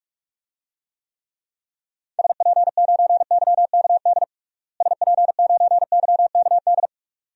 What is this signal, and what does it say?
Morse code, heard as a keyed tone, about 47 wpm, 700 Hz. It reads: SP9YKD SP9YKD
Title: DXCC-100 Call Signs - 6 Characters - Repeated 2x - 45wpm